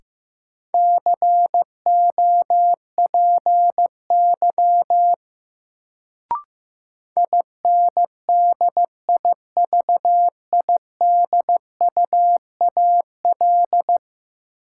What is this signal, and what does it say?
Morse code, heard as a keyed tone, about 15 wpm, 700 Hz. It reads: COPY INDIVIDUAL
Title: Sentences from Top 1400 Words - New Word Drill Alt - 15wpm